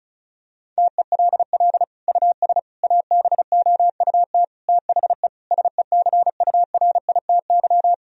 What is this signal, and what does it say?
Morse code, heard as a keyed tone, about 36 wpm, 700 Hz. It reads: TELL US ABOUT THE SECURITY
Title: Sentences from Top 1000 Words - Review Part 4 - 35wpm